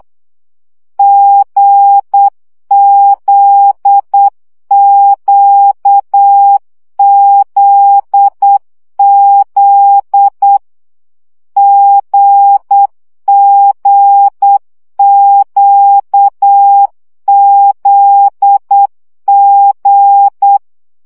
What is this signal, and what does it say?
Morse code, heard as a keyed tone, about 8 wpm, 800 Hz. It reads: GZQZZ GGQZG
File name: CW gyakorlas g z q 7 baud